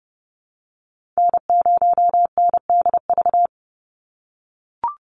Morse code, heard as a keyed tone, about 30 wpm, 700 Hz. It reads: N0NDV
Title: USA Call Signs 30wpm